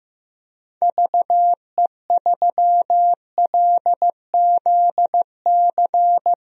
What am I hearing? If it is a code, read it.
Morse code, heard as a keyed tone, about 16 wpm, 700 Hz. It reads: VE3LZC
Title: DXCC-100 Call Signs - 6 Characters - 15wpm